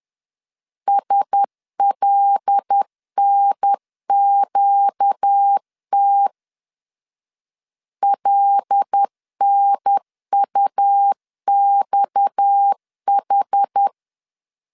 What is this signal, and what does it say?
Morse code, heard as a keyed tone, about 10 wpm, 800 Hz. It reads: SLNQT LNUXH
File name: LEZIONI / 6 PKX0 / 6 PKX 10wpm 2426